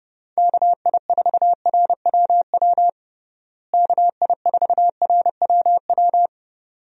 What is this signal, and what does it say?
Morse code, heard as a keyed tone, about 30 wpm, 700 Hz. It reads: KI4RWW KI4RWW
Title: USA Call Signs - 6 Characters - Repeated 2x - 30wpm